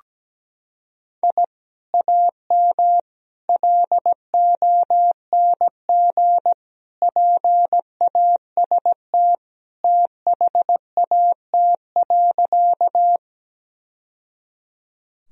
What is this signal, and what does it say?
Morse code, heard as a keyed tone, about 17 wpm, 700 Hz. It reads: I AM LONG PAST THAT.
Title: Sentences from Top 1000 Words - Review Part 3 - 17wpm